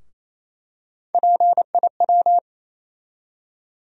Morse code, heard as a keyed tone, about 28 wpm, 700 Hz. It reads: PIW
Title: USA Call Sign Postfixes - 3 Letters 28wpm